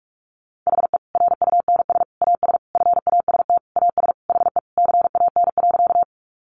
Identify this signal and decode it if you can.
Morse code, heard as a keyed tone, about 44 wpm, 700 Hz. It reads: HE RUNS AS FAST AS HE CAN.
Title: Additional Easy Phrases 45wpm